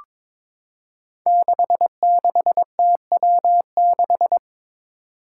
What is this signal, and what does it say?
Morse code, heard as a keyed tone, about 22 wpm, 700 Hz. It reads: 66TW6
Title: Five Challenging Letter-Number Combinations 22wpm